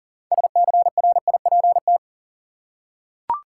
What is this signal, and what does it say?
Morse code, heard as a keyed tone, about 40 wpm, 700 Hz. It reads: SCRIPT
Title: Six Letter Words - 40wpm